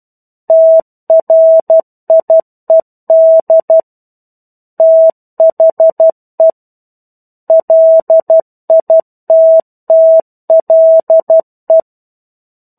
Morse code, heard as a keyed tone, about 12 wpm, 650 Hz. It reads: TRIED THE LITTLE
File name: Chapter-01 12 WPM